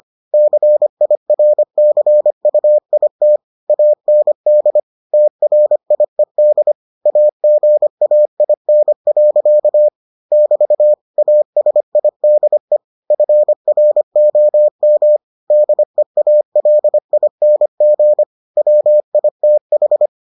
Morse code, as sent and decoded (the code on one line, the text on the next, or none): -.-. .. .-. -.-. ..- .. - / .- -. -.. / - .-. .. -.. / .- --. .- .. -. .-.-.- / -...- / .- ... .. -.. / ..-. .-. --- -- / -.. .- .-.. .. -. --. / .-- .. - ....
CIRCUIT AND TRIED AGAIN. <BT> ASIDE FROM DEALING WITH